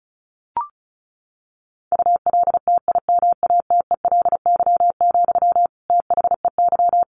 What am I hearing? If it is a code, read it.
Morse code, heard as a keyed tone, about 34 wpm, 700 Hz. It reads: ULTIMATELY, THEY
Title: Sentences from Top 1200 Words 35wpm